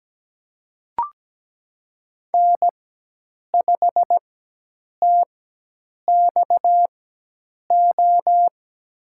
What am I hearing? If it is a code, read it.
Morse code, heard as a keyed tone, about 17 wpm, spaced as for 8 wpm, 700 Hz. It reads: N5TXO
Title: USA Call Signs - 5 Characters - 4x Character Spacing - 17wpm